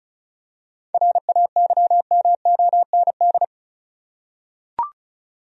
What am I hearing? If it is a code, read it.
Morse code, heard as a keyed tone, about 34 wpm, 700 Hz. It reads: RAYMOND
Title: USA Common Names - 35wpm